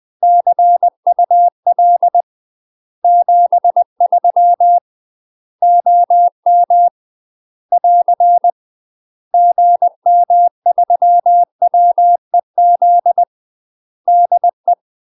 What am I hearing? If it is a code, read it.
Morse code, heard as a keyed tone, about 20 wpm, 700 Hz. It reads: CUL 73 OM <AR> GM3WEZ DE
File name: UK MASTER QSOs TRAINING / 20WPM / UK MASTER QSOs 0044